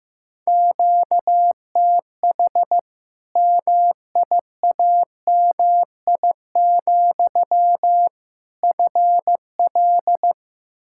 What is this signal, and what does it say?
Morse code, heard as a keyed tone, about 15 wpm, 700 Hz. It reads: QTH MIAMI, FL
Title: Practice QSO - Exchange 1 - Beginning - 15wpm